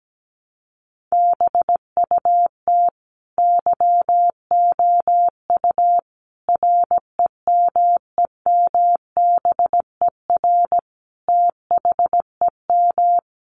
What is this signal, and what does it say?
Morse code, heard as a keyed tone, about 17 wpm, 700 Hz. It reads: BUT YOU REMEMBER THEM
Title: Sentences from Top 1300 Words 17wpm